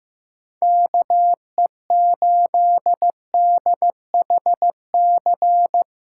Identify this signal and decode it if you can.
Morse code, heard as a keyed tone, about 15 wpm, 700 Hz. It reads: KE8DHC
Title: USA Callsign DE Callsign 15wpm